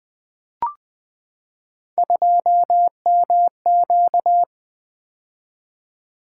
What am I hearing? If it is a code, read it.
Morse code, heard as a keyed tone, about 20 wpm, 700 Hz. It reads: 2MQ
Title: Three Letter-Number Combinations - 20wpm